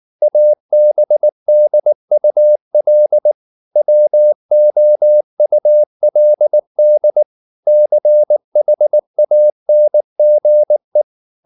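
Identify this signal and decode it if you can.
Morse code, heard as a keyed tone, about 19 wpm, 600 Hz. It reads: ABDUL WOULD CHANGE
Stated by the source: Morse code at 19 wpm from CW med Gnister, reading Ken Follett's novel Never in English.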